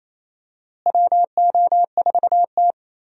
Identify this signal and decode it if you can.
Morse code, heard as a keyed tone, about 28 wpm, 700 Hz. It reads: WO4T
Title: USA Call Signs - 4 Characters - Repeated 3x - 28wpm